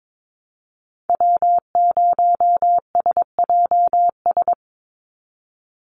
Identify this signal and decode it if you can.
Morse code, heard as a keyed tone, about 22 wpm, 700 Hz. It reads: W0SJS
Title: Call Signs - 22wpm